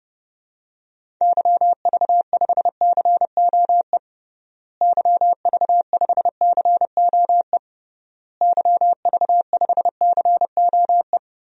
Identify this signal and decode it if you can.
Morse code, heard as a keyed tone, about 32 wpm, 700 Hz. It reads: YV5COE YV5COE YV5COE
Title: DXCC-100 Call Signs - 6 Characters - Repeated 3x - 30wpm